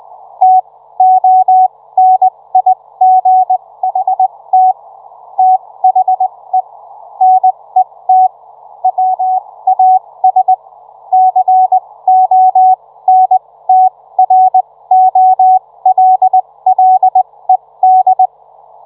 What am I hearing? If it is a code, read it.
Morse code, heard as a keyed tone, about 20 wpm, spaced as for 16 wpm, 750 Hz. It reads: TONIGHT THE NET WAS CONTROLLED